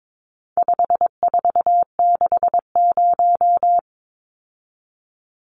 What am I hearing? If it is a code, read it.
Morse code, heard as a keyed tone, about 22 wpm, 700 Hz. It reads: 5460
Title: Numbers - 4-Digit 22wpm